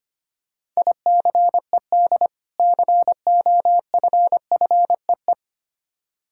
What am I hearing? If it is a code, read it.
Morse code, heard as a keyed tone, about 25 wpm, 700 Hz. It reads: ICED COFFEE
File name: practice / 25wpm / Difficult Words with Hints - 25wpm